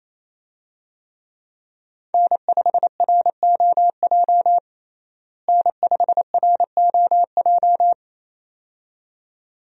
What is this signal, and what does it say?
Morse code, heard as a keyed tone, about 28 wpm, 700 Hz. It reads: N5ROJ N5ROJ
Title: USA Call Signs - 5 Characters - Repeated 2x - 28wpm